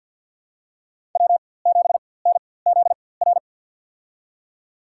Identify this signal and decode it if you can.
Morse code, heard as a keyed tone, about 52 wpm, spaced as for 23 wpm, 700 Hz. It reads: W7NZR
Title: USA Call Signs - 5 Characters - 4x Character Spacing - 50wpm